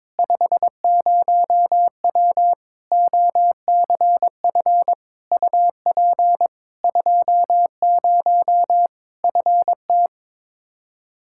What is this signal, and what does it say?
Morse code, heard as a keyed tone, about 22 wpm, 700 Hz. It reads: 50W OCF UP 20 FT
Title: Practice QSO - Exchange 2 - Info - 22wpm